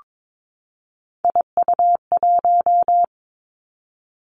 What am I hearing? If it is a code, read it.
Morse code, heard as a keyed tone, about 22 wpm, 700 Hz. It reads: IU1
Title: Three Challenging Letter-Number Combinations 22wpm